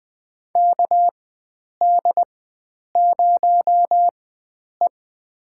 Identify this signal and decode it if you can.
Morse code, heard as a keyed tone, about 20 wpm, spaced as for 9 wpm, 700 Hz. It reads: KD0E
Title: USA Call Signs - 4x Character Spacing - 20wpm